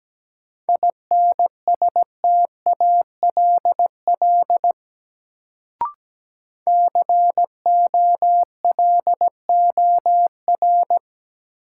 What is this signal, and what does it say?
Morse code, heard as a keyed tone, about 17 wpm, 700 Hz. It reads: INSTALL COLOR
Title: Sentences from Top 1600 Words - New Word Drill Alt - 17wpm